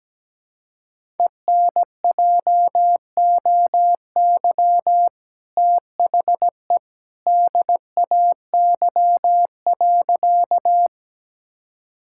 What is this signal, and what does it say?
Morse code, heard as a keyed tone, about 17 wpm, 700 Hz. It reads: ENJOY THE DAY.
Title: Sentences from Top 1000 Words - Review Part 5 - 17wpm